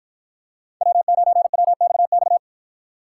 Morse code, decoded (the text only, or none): W9WXX